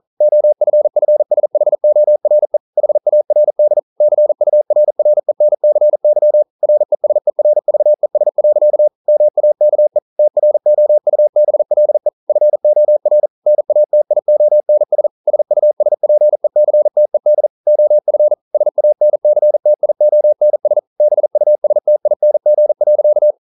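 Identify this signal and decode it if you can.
Morse code, heard as a keyed tone, about 42 wpm, 600 Hz. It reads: OFFSHORE HARD CURRENCY RESERVES. MAKE TROUBLE FOR NATIONS SUSPECTED OF SANCTIONS BUSTING.
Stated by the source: Morse code practice at 41 wpm, from CW med Gnister.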